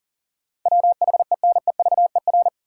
Morse code, heard as a keyed tone, about 40 wpm, 700 Hz. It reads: WHENEVER
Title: Sentences from Top 1200 Words - New Word Drill - 40wpm